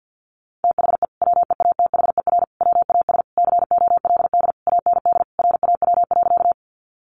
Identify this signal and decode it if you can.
Morse code, heard as a keyed tone, about 50 wpm, 700 Hz. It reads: THE WEATHER WAS COLD AND RAW.
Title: Sentences from Top 1500 Words 50wpm